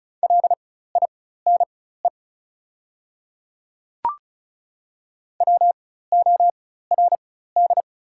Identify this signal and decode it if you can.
Morse code, heard as a keyed tone, about 36 wpm, spaced as for 16 wpm, 700 Hz. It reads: LINE WORD
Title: 4x Character Spacing - Top 300 Words - 35wpm